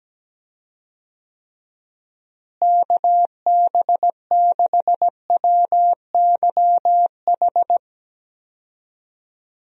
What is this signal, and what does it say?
Morse code, heard as a keyed tone, about 17 wpm, 700 Hz. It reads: KB6WYH
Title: USA Call Signs - 6 Characters - Repeated 3x - 17wpm